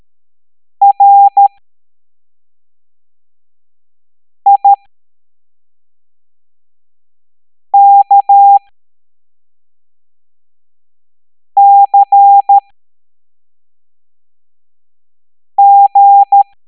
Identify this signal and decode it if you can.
Morse code, heard as a keyed tone, about 13 wpm, spaced as for 3 wpm, 800 Hz. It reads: RIKCG
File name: hamdata / L03-EX01-IGC